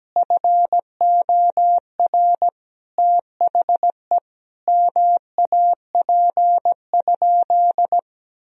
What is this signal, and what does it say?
Morse code, heard as a keyed tone, about 17 wpm, 700 Hz. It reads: FOR THE MAP?
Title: Sentences from Top 1000 Words - Review Part 2 - 17wpm